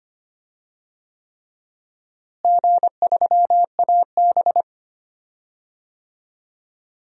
Morse code, heard as a keyed tone, about 25 wpm, 700 Hz. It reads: G3AB
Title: DXCC-100 Call Signs - 4 Characters - Repeated 3x - 25wpm